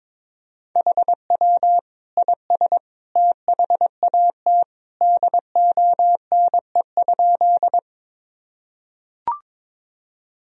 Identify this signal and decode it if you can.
Morse code, heard as a keyed tone, about 21 wpm, 700 Hz. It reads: HW IS THAT DONE?
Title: Sentences from Top 500 Words and Abbreviations 22wpm